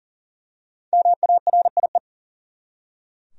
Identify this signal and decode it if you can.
Morse code, heard as a keyed tone, about 40 wpm, 700 Hz. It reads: MARIE
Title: USA Common Names - Max 5 Letters - 40wpm